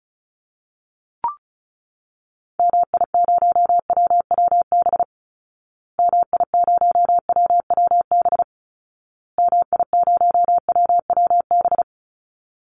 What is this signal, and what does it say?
Morse code, heard as a keyed tone, about 34 wpm, 700 Hz. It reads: MI0WWB MI0WWB MI0WWB
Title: Call Signs Repeated 2x - 35wpm